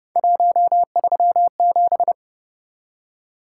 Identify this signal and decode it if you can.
Morse code, heard as a keyed tone, about 29 wpm, 700 Hz. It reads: 137